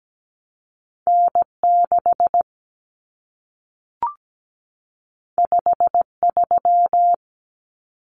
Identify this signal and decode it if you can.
Morse code, heard as a keyed tone, about 17 wpm, 700 Hz. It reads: N6 53
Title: Common Cut Numbers - 2-Digit 17wpm